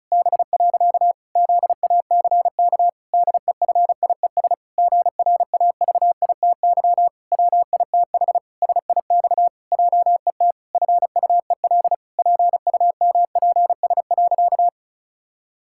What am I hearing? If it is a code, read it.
Morse code, heard as a keyed tone, about 36 wpm, 700 Hz. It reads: B. ZACK DEFIES GRAVITY WITH SIX JET FUEL PUMPS.
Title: Pangram Challenge v2 - 35wpm